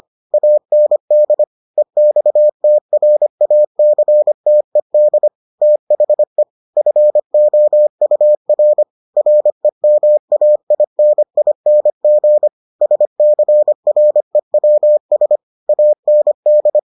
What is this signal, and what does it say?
Morse code, heard as a keyed tone, about 25 wpm, 600 Hz. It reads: AND EXTRACTED THE FOUR REMAINING SCREWS AND